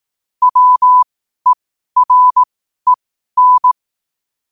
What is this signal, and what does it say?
Morse code, heard as a keyed tone, about 18 wpm, spaced as for 12 wpm, 1000 Hz. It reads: WEREN
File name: qotdfiles / 12wpm